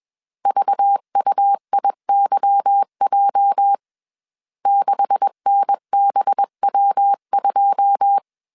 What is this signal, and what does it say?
Morse code, heard as a keyed tone, about 21 wpm, 800 Hz. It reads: VUIYJ 6NBW2
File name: CW / DET / 1 9 1500C S C 20wpm 2310